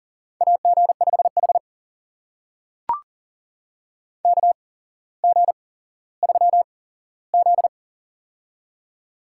Morse code, heard as a keyed tone, about 40 wpm, spaced as for 17 wpm, 700 Hz. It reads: AG5H KG3Z
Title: USA Call Signs - 4 Characters - 8x Character Spacing - 40wpm